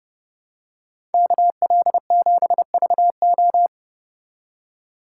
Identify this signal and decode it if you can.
Morse code, heard as a keyed tone, about 30 wpm, 700 Hz. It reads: KL7VO